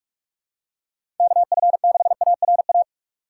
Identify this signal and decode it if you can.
Morse code, heard as a keyed tone, about 46 wpm, 700 Hz. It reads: KF6ARA
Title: USA Call Signs - 6 Characters - 6x Character Spacing - 45wpm